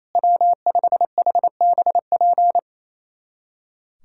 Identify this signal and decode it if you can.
Morse code, heard as a keyed tone, about 28 wpm, 700 Hz. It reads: W5HBP
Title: USA Callsign DE Callsign 28wpm